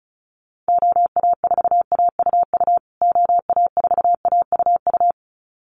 Morse code, heard as a keyed tone, about 34 wpm, 700 Hz. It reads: OA4AUU OA4AUU
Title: USA Call Signs Repeated 2x 35wpm